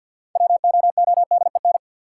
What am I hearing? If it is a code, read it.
Morse code, heard as a keyed tone, about 52 wpm, 700 Hz. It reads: WOODEN
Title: Six Letter Words - 50wpm